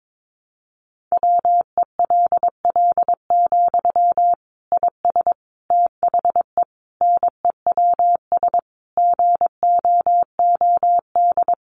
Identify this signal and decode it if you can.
Morse code, heard as a keyed tone, about 22 wpm, 700 Hz. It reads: WELL, IS THE NEWS GOOD
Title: Sentences from Top 800 Words 22wpm